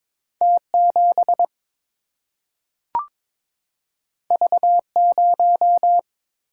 Morse code, heard as a keyed tone, about 22 wpm, 700 Hz. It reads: T7 V0
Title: Single-Letter Single-Number 22wpm